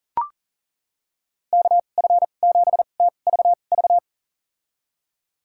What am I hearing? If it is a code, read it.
Morse code, heard as a keyed tone, about 40 wpm, spaced as for 29 wpm, 700 Hz. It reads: KF7TVV
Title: USA Call Signs - 6 Characters - 2x Character Spacing - 40wpm